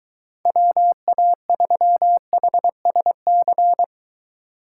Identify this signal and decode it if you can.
Morse code, heard as a keyed tone, about 21 wpm, 700 Hz. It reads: WA3HSC